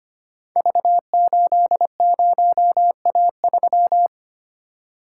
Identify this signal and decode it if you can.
Morse code, heard as a keyed tone, about 25 wpm, 700 Hz. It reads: V80A3
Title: Code Groups - 5 Letter:Numbers - 25wpm